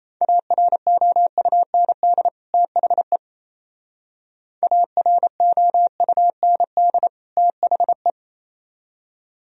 Morse code, heard as a keyed tone, about 36 wpm, 700 Hz. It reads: AROUND THE AROUND THE